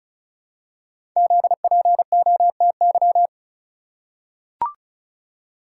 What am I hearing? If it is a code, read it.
Morse code, heard as a keyed tone, about 36 wpm, 700 Hz. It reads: ZPOTY
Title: Code Groups - 5 Letters - 35wpm